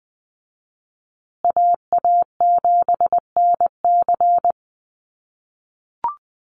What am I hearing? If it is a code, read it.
Morse code, heard as a keyed tone, about 20 wpm, 700 Hz. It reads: AA7NC
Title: USA Call Signs Repeated 3x 20wpm